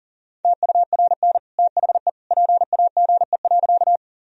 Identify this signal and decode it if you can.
Morse code, heard as a keyed tone, about 40 wpm, 700 Hz. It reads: TURN THE PAGE.
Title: Sentences from Top 1000 Words - Review Part 1 - 40wpm